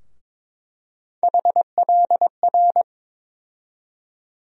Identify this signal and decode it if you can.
Morse code, heard as a keyed tone, about 22 wpm, 700 Hz. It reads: HLR
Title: USA Call Sign Postfixes - 3 Letters 22wpm